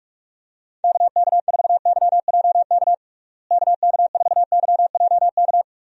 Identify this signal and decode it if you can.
Morse code, heard as a keyed tone, about 46 wpm, 700 Hz. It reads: KK4YJK KK4YJK
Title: USA Call Signs - 6 Characters - Repeated 2x - 45wpm